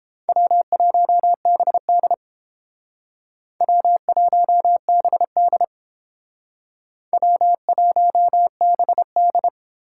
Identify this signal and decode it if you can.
Morse code, heard as a keyed tone, about 31 wpm, 700 Hz. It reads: W1BD W1BD W1BD